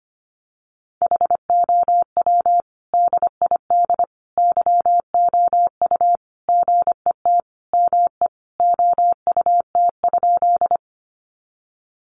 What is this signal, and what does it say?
Morse code, heard as a keyed tone, about 25 wpm, 700 Hz. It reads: HOW DID YOU GET ME OUT?
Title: Sentences from Top 200 Words 25wpm